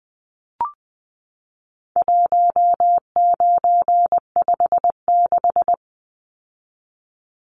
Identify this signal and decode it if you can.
Morse code, heard as a keyed tone, about 20 wpm, 700 Hz. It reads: 1956
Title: Numbers - 4-Digit 20wpm